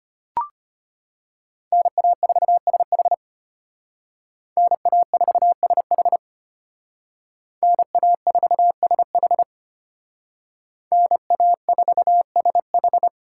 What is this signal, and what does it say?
Morse code, heard as a keyed tone, about 32 wpm, 700 Hz. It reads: NA4SH NA4SH NA4SH NA4SH